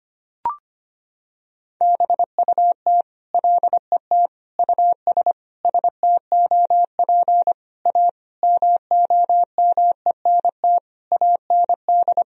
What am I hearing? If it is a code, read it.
Morse code, heard as a keyed tone, about 25 wpm, 700 Hz. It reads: BUT LET US STOP A MOMENT AND
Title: Sentences from Top 1000 Words - Review Part 1 - 25wpm